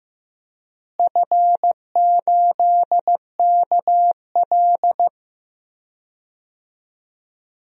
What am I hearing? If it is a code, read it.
Morse code, heard as a keyed tone, about 15 wpm, 700 Hz. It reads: F8KL
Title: DXCC-100 Call Signs - 4 Characters - Repeated 3x - 15wpm